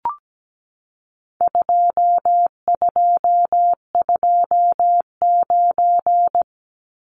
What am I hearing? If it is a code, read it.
Morse code, heard as a keyed tone, about 17 wpm, 700 Hz. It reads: 2229
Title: Numbers - 4-Digit 17wpm